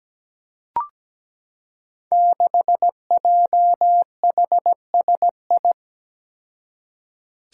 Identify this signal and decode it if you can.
Morse code, heard as a keyed tone, about 17 wpm, 700 Hz. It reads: 6JHSI
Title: Five Challenging Letter-Number Combinations 17wpm